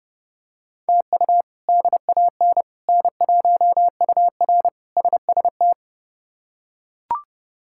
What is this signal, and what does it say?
Morse code, heard as a keyed tone, about 30 wpm, 700 Hz. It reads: TU DAN N1UR SST
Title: K1USN SST QSO Practice - 30wpm